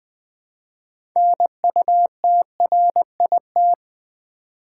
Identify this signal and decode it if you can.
Morse code, heard as a keyed tone, about 20 wpm, 700 Hz. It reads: NUTRIT
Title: Build A Word - 3 Letters At A Time - 9-Letter Words - 20wpm